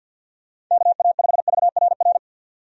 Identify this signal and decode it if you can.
Morse code, heard as a keyed tone, about 49 wpm, 700 Hz. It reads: KA5VRR
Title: USA Call Signs - 6 Characters - 50wpm